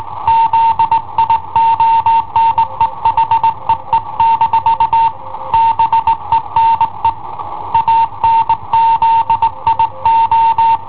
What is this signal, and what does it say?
Morse code, heard as a keyed tone, about 19 wpm, 910 Hz. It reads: ZIONE HEE- BENE ANZIO
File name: i7ohp / cassetta i7ohp B